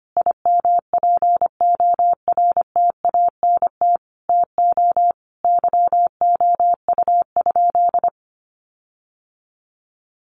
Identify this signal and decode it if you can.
Morse code, heard as a keyed tone, about 25 wpm, 700 Hz. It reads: IMPORTANT TO YOU?
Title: Sentences from Top 1300 Words 25wpm